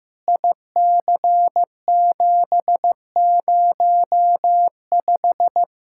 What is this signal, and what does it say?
Morse code, heard as a keyed tone, about 15 wpm, 700 Hz. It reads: IC705
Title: K1USN SST QSO Practice 15wpm